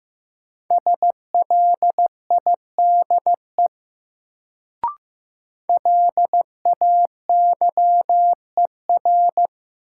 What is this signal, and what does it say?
Morse code, heard as a keyed tone, about 15 wpm, 700 Hz. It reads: SLIDE LAYER
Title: Sentences from Top 1600 Words - New Word Drill Alt - 15wpm